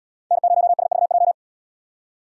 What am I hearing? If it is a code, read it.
Morse code, heard as a keyed tone, about 75 wpm, 700 Hz. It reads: N0SVJ